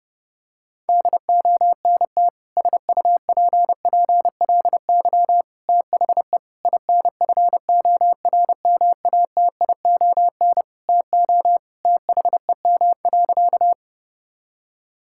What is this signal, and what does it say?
Morse code, heard as a keyed tone, about 30 wpm, 700 Hz. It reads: DONT SUPPLY THE INFORMATION TO THEM.
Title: Sentences from Top 1000 Words - Review Part 5 - 30wpm